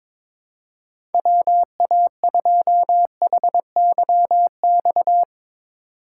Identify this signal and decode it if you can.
Morse code, heard as a keyed tone, about 22 wpm, 700 Hz. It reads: WA2HYX
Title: USA Call Signs - 6 Characters - 4x Character Spacing - 22wpm